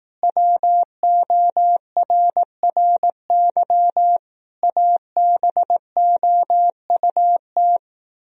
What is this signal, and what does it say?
Morse code, heard as a keyed tone, about 18 wpm, 700 Hz. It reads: WORRY ABOUT
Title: Speed-Racing - Sets of 5 Words 15wpm